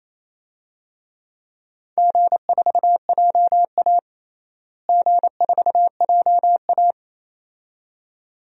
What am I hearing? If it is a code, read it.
Morse code, heard as a keyed tone, about 28 wpm, 700 Hz. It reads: G4JA G4JA
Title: DXCC-100 Call Signs - 4 Characters - Repeated 2x - 28wpm